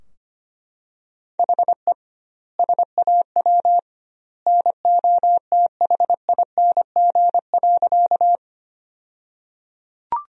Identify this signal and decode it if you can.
Morse code, heard as a keyed tone, about 25 wpm, 700 Hz. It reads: HE SAW NOTHING.
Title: 2x Word-Spacing - Sentences from Top 500 Words - 25wpm